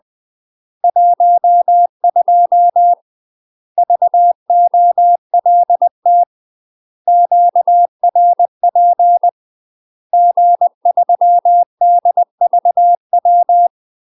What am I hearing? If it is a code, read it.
Morse code, heard as a keyed tone, about 20 wpm, 700 Hz. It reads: 12 VOLT QRP G3DVW
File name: UK MASTER QSOs TRAINING / 20WPM / UK MASTER QSOs 0037